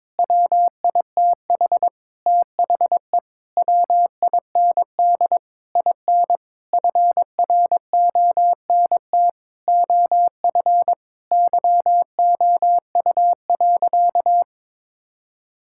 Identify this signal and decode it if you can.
Morse code, heard as a keyed tone, about 22 wpm, 700 Hz. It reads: WITH THE WIND IN FRONT OF YOU.
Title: Sentences from Top 1000 Words - Review Part 6 - 22wpm